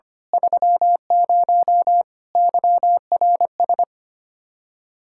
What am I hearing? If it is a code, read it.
Morse code, heard as a keyed tone, about 29 wpm, 700 Hz. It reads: 30 YRS
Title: Speed-Racing Common QSO Phrases Encore - 25wpm